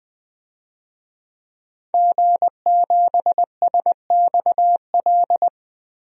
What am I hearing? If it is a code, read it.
Morse code, heard as a keyed tone, about 20 wpm, 700 Hz. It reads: G7SXL
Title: DXCC-100 Call Signs - 5 Characters - 20wpm